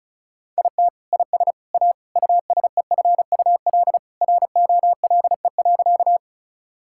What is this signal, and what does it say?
Morse code, heard as a keyed tone, about 36 wpm, 700 Hz. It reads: IT IS A USEFUL ROLE.
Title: Sentences from Top 1000 Words - Review Part 6 - 35wpm